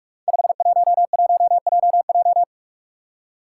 Morse code, heard as a keyed tone, about 44 wpm, 700 Hz. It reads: 511JJ
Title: Five Challenging Letter-Number Combinations 45wpm